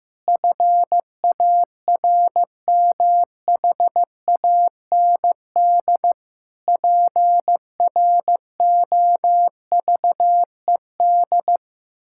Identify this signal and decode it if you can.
Morse code, heard as a keyed tone, about 15 wpm, 700 Hz. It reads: FARMHAND PROVED
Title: Pangram Challenge v2 - 15wpm